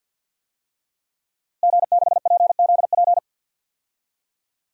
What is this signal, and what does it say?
Morse code, heard as a keyed tone, about 53 wpm, 700 Hz. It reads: G6PZP
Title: DXCC-100 Call Signs - 5 Characters - Repeated 2x - 50wpm